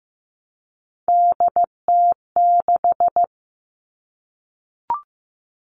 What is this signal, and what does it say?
Morse code, heard as a keyed tone, about 15 wpm, 700 Hz. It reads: DT6
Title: All Cut Numbers - 3-Digit 15wpm